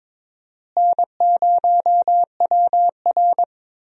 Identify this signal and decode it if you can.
Morse code, heard as a keyed tone, about 22 wpm, 700 Hz. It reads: N0WR
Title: USA Call Signs - 4 Characters - Repeated 2x - 22wpm